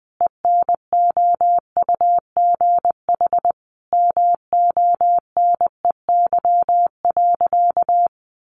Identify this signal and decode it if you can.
Morse code, heard as a keyed tone, about 20 wpm, 700 Hz. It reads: ENOUGH MONEY.
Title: Additional Easy Phrases 20wpm